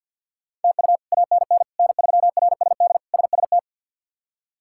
Morse code, heard as a keyed tone, about 53 wpm, 700 Hz. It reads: TU ANN N3RSD SST
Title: K1USN SST QSO Practice - 50wpm